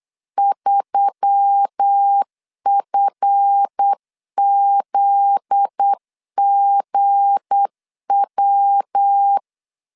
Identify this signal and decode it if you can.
Morse code, heard as a keyed tone, about 8 wpm, 800 Hz. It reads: 3FZGW
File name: LEZIONI / 7 GF9Z / 7 GF9Z 8wpm 3213